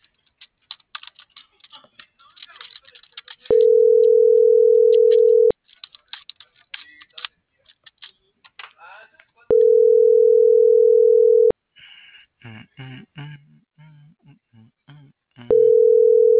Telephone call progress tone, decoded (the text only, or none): ringback tone